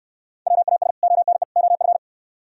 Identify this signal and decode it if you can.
Morse code, heard as a keyed tone, about 61 wpm, 700 Hz. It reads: WAS ONE OF